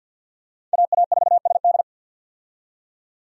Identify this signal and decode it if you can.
Morse code, heard as a keyed tone, about 53 wpm, 700 Hz. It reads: AA4SD
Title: USA Call Signs - 5 Characters - 50wpm